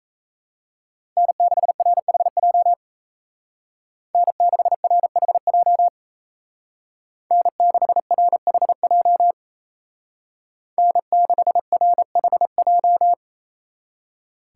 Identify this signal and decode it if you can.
Morse code, heard as a keyed tone, about 37 wpm, 700 Hz. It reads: N6RHJ N6RHJ N6RHJ N6RHJ